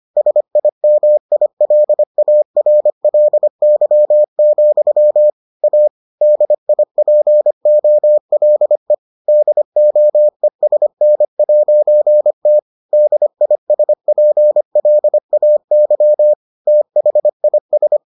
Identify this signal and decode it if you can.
Morse code, heard as a keyed tone, about 25 wpm, 600 Hz. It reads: SIMILARLY, A DIPOLE DOESN'T DISPLAY THIS